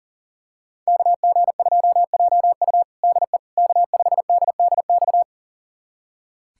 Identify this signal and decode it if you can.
Morse code, heard as a keyed tone, about 37 wpm, 700 Hz. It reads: KG2JU DE K5DDX
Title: USA Callsign DE Callsign - 5 Characters - 40wpm